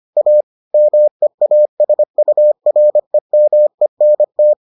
Morse code, heard as a keyed tone, about 25 wpm, 600 Hz. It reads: A MEASUREMENT